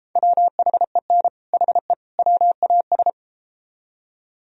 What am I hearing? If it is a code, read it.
Morse code, heard as a keyed tone, about 32 wpm, 700 Hz. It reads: WHEN HE WAS